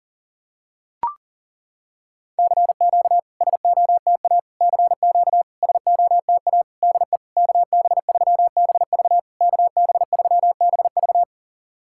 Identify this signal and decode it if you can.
Morse code, heard as a keyed tone, about 40 wpm, 700 Hz. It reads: CQ SOTA CQ SOTA DE KB3BV KB3BV
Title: SOTA QSO Practice 40wpm